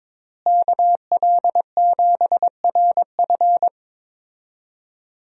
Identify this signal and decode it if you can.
Morse code, heard as a keyed tone, about 22 wpm, 700 Hz. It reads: KL7RF
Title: DXCC-100 Call Signs - 5 Characters - Repeated 2x - 22wpm